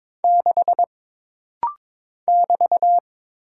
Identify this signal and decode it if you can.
Morse code, heard as a keyed tone, about 21 wpm, 700 Hz. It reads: 6 <BT>
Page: This rapid-fire timing is geared to push your ICR (Instant Character Recognition) skills.